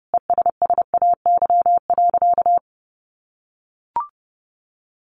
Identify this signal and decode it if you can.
Morse code, heard as a keyed tone, about 30 wpm, 700 Hz. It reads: ESSAY.
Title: Sentences from Top 1500 Words 30wpm